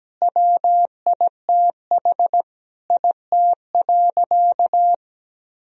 Morse code, heard as a keyed tone, about 17 wpm, 700 Hz. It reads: WITH IT.
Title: Sentences from Top 1000 Words - Review Part 1 - 17wpm